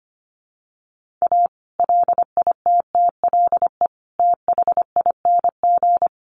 Morse code, heard as a keyed tone, about 25 wpm, 700 Hz. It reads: A LITTLE THING
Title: Sentences from Top 1500 Words 25wpm